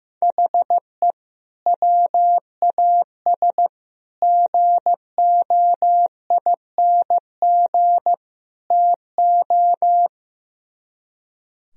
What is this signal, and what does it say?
Morse code, heard as a keyed tone, about 15 wpm, 700 Hz. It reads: HE WAS GOING TO
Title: Speed-Racing - Sets of 4 Words 15wpm